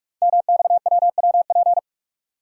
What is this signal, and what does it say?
Morse code, heard as a keyed tone, about 44 wpm, 700 Hz. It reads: MXWWP
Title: Code Groups - 5 Letters - 45wpm